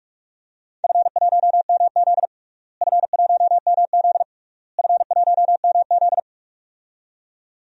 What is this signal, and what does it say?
Morse code, heard as a keyed tone, about 46 wpm, 700 Hz. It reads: F1MZ F1MZ F1MZ
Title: DXCC-100 Call Signs - 4 Characters - Repeated 3x - 45wpm